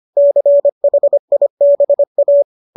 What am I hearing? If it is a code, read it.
Morse code, heard as a keyed tone, about 25 wpm, 570 Hz. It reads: CHIBA
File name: qso208 25